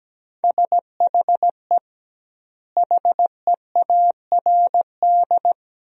Morse code, heard as a keyed tone, about 17 wpm, 700 Hz. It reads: SHE HEARD
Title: 2x Word-Spacing - Sentences from Top 500 Words - 17wpm